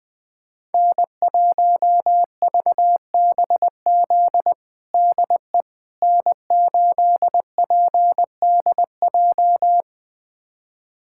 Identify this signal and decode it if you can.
Morse code, heard as a keyed tone, about 20 wpm, 700 Hz. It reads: N1VBZ DE N8PDJ
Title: USA Callsign DE Callsign - 5 Characters - 20wpm